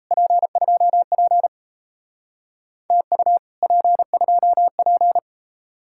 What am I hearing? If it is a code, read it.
Morse code, heard as a keyed tone, about 35 wpm, 700 Hz. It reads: P2P TU P2P
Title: Speed-Racing Common QSO Phrases - 28wpm